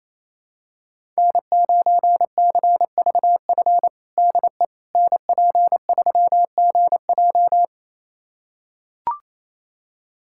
Morse code, heard as a keyed tone, about 28 wpm, 700 Hz. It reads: N9CVF DE NP3GJ
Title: USA Callsign DE Callsign - 5 Characters - 28wpm